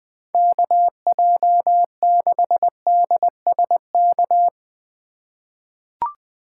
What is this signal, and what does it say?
Morse code, heard as a keyed tone, about 20 wpm, 700 Hz. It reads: KJ6DSK
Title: USA Call Signs - 3x Character Spacing - 20wpm